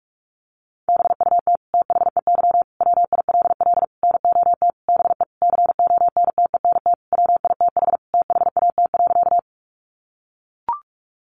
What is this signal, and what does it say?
Morse code, heard as a keyed tone, about 44 wpm, 700 Hz. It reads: BUT THEY WILL NOT BE CONTENT WITH THAT.
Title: Sentences from Top 900 Words 45wpm